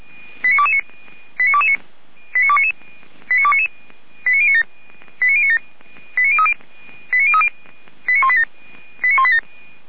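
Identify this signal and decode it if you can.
Five-tone selective call (ZVEI), heard as quick runs of five tones, tones of about 70 ms. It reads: 89229 89229 89299 89299 89097 89097 89030 89030 89187 89187
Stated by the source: ZVEI five-tone call received by SDR, fire brigade Orlamünde (feuerwehr-orlamuende).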